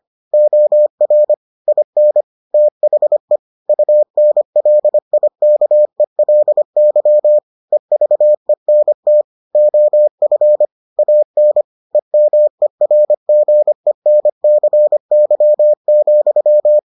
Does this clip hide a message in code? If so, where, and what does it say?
Yes: Morse code from 0.3 to 16.8 s, heard as a keyed tone, about 25 wpm, 600 Hz. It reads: OR IN THE UNLIKELY EVENT OF AN EMERGENCY,